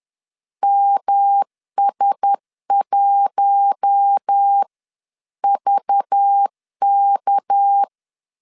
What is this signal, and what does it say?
Morse code, heard as a keyed tone, about 10 wpm, spaced as for 9 wpm, 800 Hz. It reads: MS1VK
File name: LEZIONI / 6 PKX0 / 6 PK 10wpm 2436